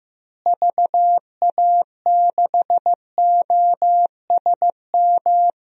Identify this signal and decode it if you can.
Morse code, heard as a keyed tone, about 15 wpm, 700 Hz. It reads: VA6OSM
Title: DXCC-100 Call Signs - 6 Characters - Repeated 2x - 15wpm